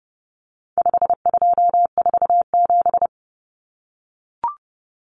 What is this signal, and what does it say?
Morse code, heard as a keyed tone, about 30 wpm, 700 Hz. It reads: 5247
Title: Common Cut Numbers - 4-Digit 30wpm